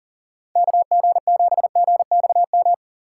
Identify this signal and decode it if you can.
Morse code, heard as a keyed tone, about 40 wpm, 690 Hz. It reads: KG7GXM
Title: USA Call Signs - 6 Characters - 2x Character Spacing - 40wpm